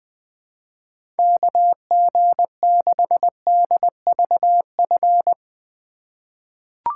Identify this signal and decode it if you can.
Morse code, heard as a keyed tone, about 20 wpm, 700 Hz. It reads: KG6DVF
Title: USA Call Signs - 3x Character Spacing - 20wpm